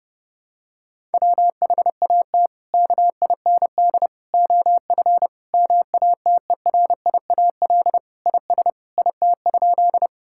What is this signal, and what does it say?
Morse code, heard as a keyed tone, about 30 wpm, 700 Hz. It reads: WHAT KIND OF MATERIAL IS IT?
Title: Sentences from Top 1000 Words - Review Part 5 - 30wpm